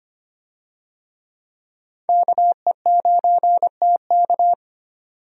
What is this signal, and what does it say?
Morse code, heard as a keyed tone, about 25 wpm, 700 Hz. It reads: KE9TK
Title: USA Call Signs - 5 Characters - 25wpm